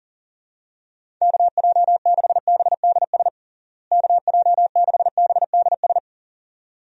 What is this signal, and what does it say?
Morse code, heard as a keyed tone, about 40 wpm, 700 Hz. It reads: KJ6BDS KJ6BDS
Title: USA Call Signs - 6 Characters - Repeated 2x - 40wpm